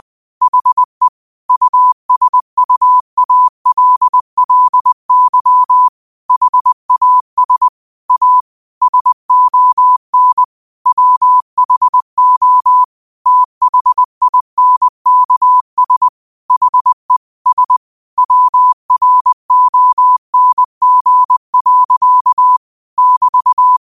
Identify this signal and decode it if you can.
Morse code, heard as a keyed tone, about 20 wpm, 1000 Hz. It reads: HE USUALLY HAS A SON WHO THINKS HE S WRONG. <BT>
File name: qotdfiles / 20wpm